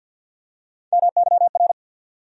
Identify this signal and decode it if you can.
Morse code, heard as a keyed tone, about 48 wpm, 700 Hz. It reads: MYR
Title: USA Call Sign Postfixes - 3 Letters 50wpm